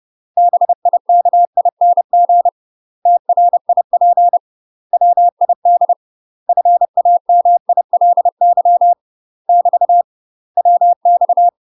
Morse code, heard as a keyed tone, about 30 wpm, 700 Hz. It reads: BIKING TRIP WID FAMILY <BT> WX
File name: UK MASTER QSOs TRAINING / 30WPM / UK MASTER QSOs 0013